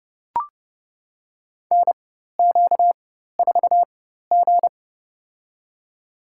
Morse code, heard as a keyed tone, about 32 wpm, spaced as for 14 wpm, 700 Hz. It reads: NQ4G
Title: USA Call Signs - 4x Character Spacing - 30wpm